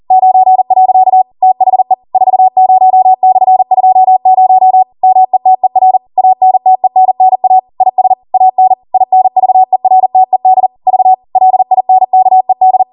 Morse code, heard as a keyed tone, about 40 wpm, 750 Hz. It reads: 9. THE 40/20 METER ANTENNA IS AN INVERTED V LINKED
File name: Morse / 40WPM